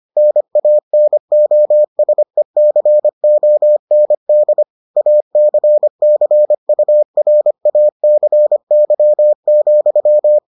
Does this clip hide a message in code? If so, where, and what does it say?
Yes: Morse code from 0.2 to 10.4 s, heard as a keyed tone, about 25 wpm, 600 Hz. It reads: NANOSECOND ACCURACY,